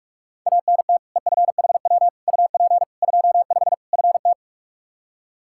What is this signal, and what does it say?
Morse code, heard as a keyed tone, about 46 wpm, 700 Hz. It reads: ANT EFHW UP 25 FT
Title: K1USN SST QSO Practice 45wpm